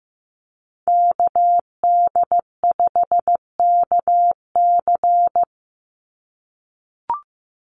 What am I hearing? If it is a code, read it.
Morse code, heard as a keyed tone, about 15 wpm, 700 Hz. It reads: KD5KC
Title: NA SOTA Activator Callsigns 15wpm